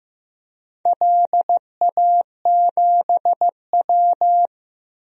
Morse code, heard as a keyed tone, about 15 wpm, 700 Hz. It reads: LA7W
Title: DXCC-100 Call Signs - 4 Characters - Repeated 3x - 15wpm